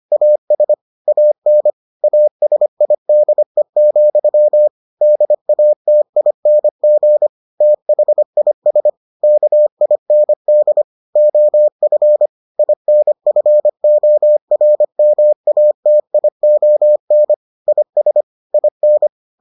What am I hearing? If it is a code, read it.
Morse code, heard as a keyed tone, about 25 wpm, 600 Hz. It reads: AS AN ASIDE, DATING THIS KIND OF INFORMATION IS IN